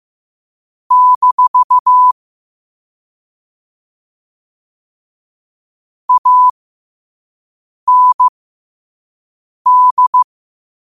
Morse code, heard as a keyed tone, about 15 wpm, spaced as for 5 wpm, 1000 Hz. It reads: - AND
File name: qotdfiles / 5wpm